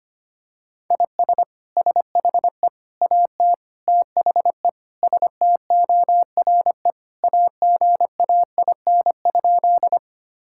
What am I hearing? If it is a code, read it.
Morse code, heard as a keyed tone, about 25 wpm, 700 Hz. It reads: IS SHE AT THE STORE AGAIN?
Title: Sentences from Top 1000 Words - Review Part 5 - 25wpm